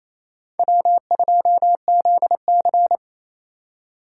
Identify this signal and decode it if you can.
Morse code, heard as a keyed tone, about 28 wpm, 700 Hz. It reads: W2ZC
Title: USA Call Signs - 4 Characters - 2x Character Spacing - 28wpm